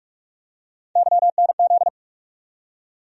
Morse code, heard as a keyed tone, about 46 wpm, 700 Hz. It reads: YNZ
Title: USA Call Sign Postfixes - 3 Letters 45wpm